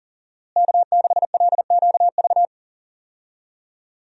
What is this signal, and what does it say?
Morse code, heard as a keyed tone, about 40 wpm, 700 Hz. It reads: K6LQV
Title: USA Call Signs - 5 Characters - Repeated 2x - 40wpm